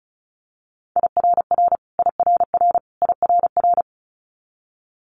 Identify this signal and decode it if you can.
Morse code, heard as a keyed tone, about 36 wpm, 700 Hz. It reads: IRR IRR IRR
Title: Multi-letter Phonemes Repeated 3x 35wpm